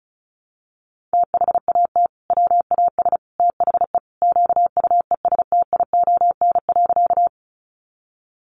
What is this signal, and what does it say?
Morse code, heard as a keyed tone, about 34 wpm, 700 Hz. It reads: THAT WAS THE QUESTION.
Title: Sentences from Top 400 Words 35wpm